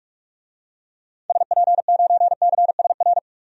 Morse code, heard as a keyed tone, about 44 wpm, 700 Hz. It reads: SP9CSR
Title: DXCC-100 Call Signs - 6 Characters - 45wpm